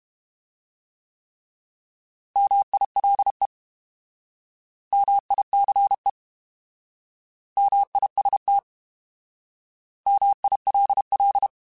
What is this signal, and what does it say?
Morse code, heard as a keyed tone, about 32 wpm, 800 Hz. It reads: MILE MICE MIST MILL
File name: Alphabetical / Alphabetical mp3s / Ms @30